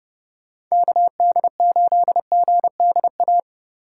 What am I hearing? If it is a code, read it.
Morse code, heard as a keyed tone, about 39 wpm, 700 Hz. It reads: KD8GDA